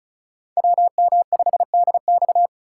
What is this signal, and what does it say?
Morse code, heard as a keyed tone, about 34 wpm, 700 Hz. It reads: WM5DX
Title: K1USN SST QSO Practice - 35wpm